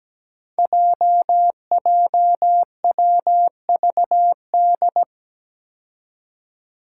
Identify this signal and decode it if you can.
Morse code, heard as a keyed tone, about 17 wpm, 700 Hz. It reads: JJWVD
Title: Code Groups - 5 Letters - 17wpm